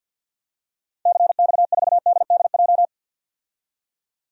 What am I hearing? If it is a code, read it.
Morse code, heard as a keyed tone, about 51 wpm, 700 Hz. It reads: CX4DDJ
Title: DXCC-100 Call Signs - 6 Characters - 50wpm